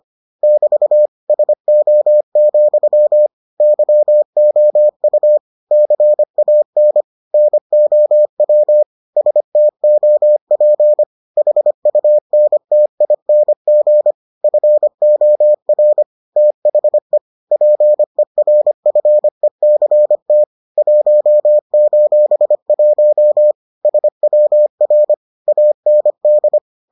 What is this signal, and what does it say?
Morse code, heard as a keyed tone, about 25 wpm, 600 Hz. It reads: <BT> SO, YOU CAN NOW STOP HUNTING FOR THE PERFECT 1:1 SWR AND